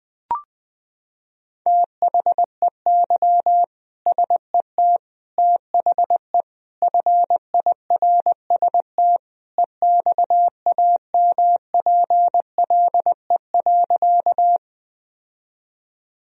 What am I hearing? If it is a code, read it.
Morse code, heard as a keyed tone, about 20 wpm, 700 Hz. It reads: THEY SET THE FIRST EXAMPLE.
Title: Sentences from Top 1000 Words - Review Part 1 - 20wpm